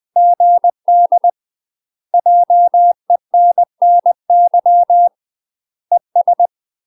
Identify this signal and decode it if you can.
Morse code, heard as a keyed tone, about 20 wpm, 700 Hz. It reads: GD JENNY ES
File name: UK MASTER QSOs TRAINING / 20WPM / UK MASTER QSOs 0011